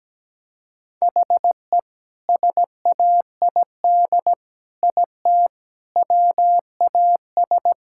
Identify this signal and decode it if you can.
Morse code, heard as a keyed tone, about 17 wpm, 700 Hz. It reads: HE SAID IT WAS
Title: Sentences from Top 1600 Words 17wpm